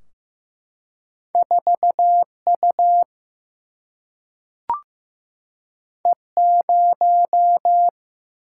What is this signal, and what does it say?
Morse code, heard as a keyed tone, about 15 wpm, 700 Hz. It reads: 4U E0
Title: Single-Letter Single-Number 15wpm